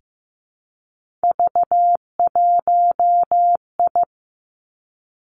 Morse code, heard as a keyed tone, about 15 wpm, 700 Hz. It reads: V1I
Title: Three Challenging Letter-Number Combinations 15wpm